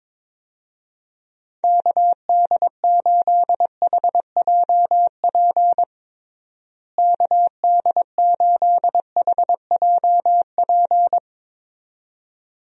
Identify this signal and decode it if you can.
Morse code, heard as a keyed tone, about 21 wpm, 700 Hz. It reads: KD8HJP KD8HJP
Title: USA Call Signs - 6 Characters - Repeated 2x - 22wpm